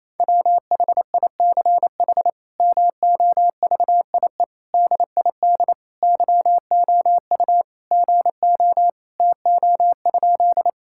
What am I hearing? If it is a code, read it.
Morse code, heard as a keyed tone, about 28 wpm, 700 Hz. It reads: WHICH MOVIE DID YOU GO TO?
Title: Sentences from Top 1000 Words - Review Part 5 - 28wpm